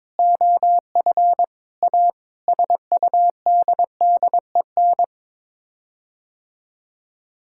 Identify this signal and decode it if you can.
Morse code, heard as a keyed tone, about 21 wpm, 700 Hz. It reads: OF A SUDDEN
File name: practice / 22wpm / Sets of 6 Words - 22wpm